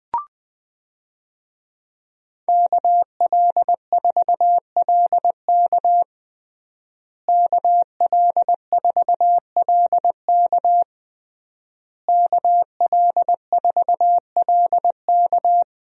Morse code, heard as a keyed tone, about 20 wpm, 700 Hz. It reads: KL4LK KL4LK KL4LK
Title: USA Call Signs - 5 Characters - Repeated 3x - 20wpm